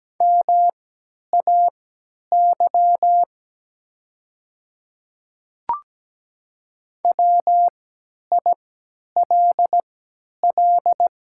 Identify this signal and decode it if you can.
Morse code, heard as a keyed tone, about 17 wpm, spaced as for 10 wpm, 700 Hz. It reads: MAY WILL
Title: 3x Character Spacing - Top 200 Words - 17wpm